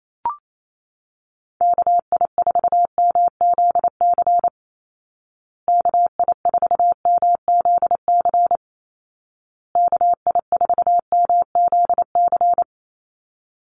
Morse code, heard as a keyed tone, about 28 wpm, 700 Hz. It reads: KI4MZC KI4MZC KI4MZC
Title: Call Signs Repeated 2x - 28wpm